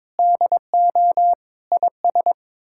Morse code, heard as a keyed tone, about 21 wpm, 700 Hz. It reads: DO IS
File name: practice / 22wpm / Sets of 6 Words - 22wpm